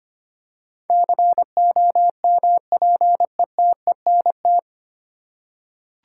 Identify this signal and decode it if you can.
Morse code, heard as a keyed tone, about 25 wpm, 700 Hz. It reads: COMPETENT
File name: Nine Letter Words - 25wpm